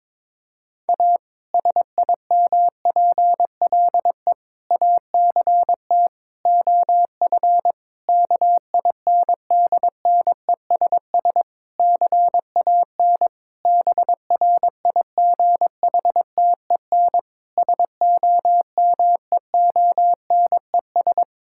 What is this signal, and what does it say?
Morse code, heard as a keyed tone, about 22 wpm, 700 Hz. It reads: A SIMPLE ACT OF KINDNESS CAN BRIGHTEN SOMEONES
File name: Difficult Sentences with Hints - 22wpm